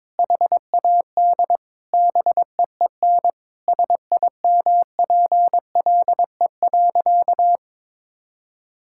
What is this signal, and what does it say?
Morse code, heard as a keyed tone, about 22 wpm, 700 Hz. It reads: HAD BEEN SIMPLE.
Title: Sentences from Top 1000 Words - Review Part 6 - 22wpm